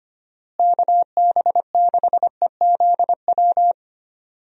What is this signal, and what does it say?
Morse code, heard as a keyed tone, about 25 wpm, 700 Hz. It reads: KB6EZW
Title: USA Call Signs - 6 Characters - Repeated 2x - 25wpm